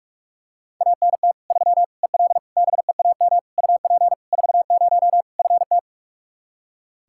Morse code, heard as a keyed tone, about 46 wpm, 700 Hz. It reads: ANT 3 EL BEAM UP 40 FT
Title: K1USN SST QSO Practice 45wpm